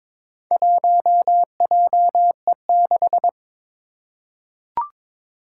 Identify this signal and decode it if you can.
Morse code, heard as a keyed tone, about 22 wpm, 700 Hz. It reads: 1JE6
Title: Four Challenging Letter-Number Combinations 22wpm